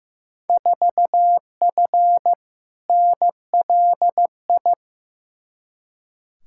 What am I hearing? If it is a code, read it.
Morse code, heard as a keyed tone, about 15 wpm, 700 Hz. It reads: 4F NLI
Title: Field Day Exchanges Encore - 15wpm